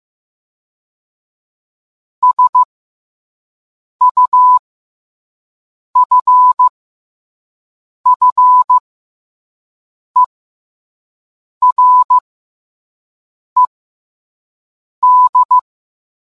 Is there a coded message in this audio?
Yes: Morse code from 2.2 to 15.6 s, heard as a keyed tone, about 15 wpm, spaced as for 5 wpm, 1000 Hz. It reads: SUFFERED